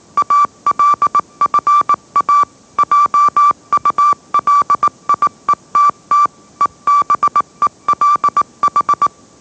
Morse code, heard as a keyed tone, about 18 wpm, 1160 Hz. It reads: ALFA JULIETT EBELH